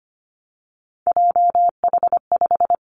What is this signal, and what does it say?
Morse code, heard as a keyed tone, about 25 wpm, 700 Hz. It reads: JH5
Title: Three Challenging Letter-Number Combinations 25wpm